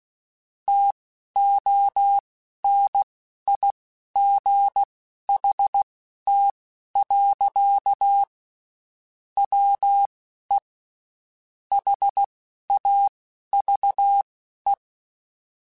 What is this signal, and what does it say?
Morse code, heard as a keyed tone, about 16 wpm, spaced as for 12 wpm, 800 Hz. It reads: TONIGHT. WE HAVE
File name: QSOs / QSOs-mp3s / WD0BGP 1@15